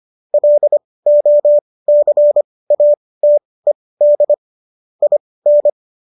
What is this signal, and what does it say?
Morse code, heard as a keyed tone, about 25 wpm, spaced as for 18 wpm, 590 Hz. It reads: LOCATED IN
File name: qso218 18